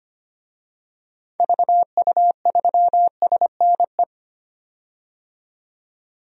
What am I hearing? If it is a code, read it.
Morse code, heard as a keyed tone, about 25 wpm, 700 Hz. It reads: VU3SNE
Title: DXCC-100 Call Signs - 6 Characters - 25wpm